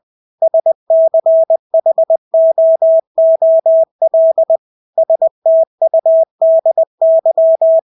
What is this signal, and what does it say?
Morse code, heard as a keyed tone, about 20 wpm, 650 Hz. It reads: SCHOOL STUDY
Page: Engelske ord 20 wpm